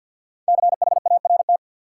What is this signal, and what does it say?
Morse code, heard as a keyed tone, about 49 wpm, 700 Hz. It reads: CHART
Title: Sentences from Top 1300 Words - New Word Drill Alt - 50wpm